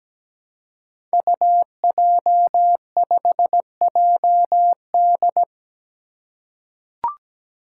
Morse code, heard as a keyed tone, about 17 wpm, 700 Hz. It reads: UJ5JD
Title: Code Groups - 5 Letter:Numbers - 17wpm